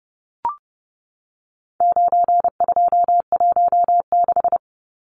Morse code, heard as a keyed tone, about 30 wpm, 700 Hz. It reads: 9216
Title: Numbers - 4-Digit 30wpm